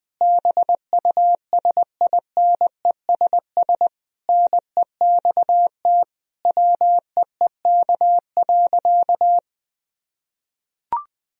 Morse code, heard as a keyed tone, about 20 wpm, 700 Hz. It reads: BUSINESS NEXT WEEK.
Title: Sentences from Top 1000 Words - Review Part 4 - 20wpm